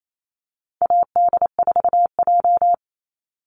Morse code, heard as a keyed tone, about 28 wpm, 700 Hz. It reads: AD4J
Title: Call Signs - 28wpm